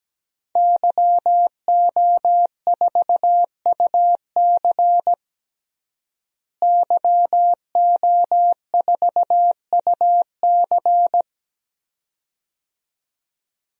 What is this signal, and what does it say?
Morse code, heard as a keyed tone, about 17 wpm, 700 Hz. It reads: YO4UC YO4UC
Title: DXCC-100 Call Signs - 5 Characters - Repeated 2x - 17wpm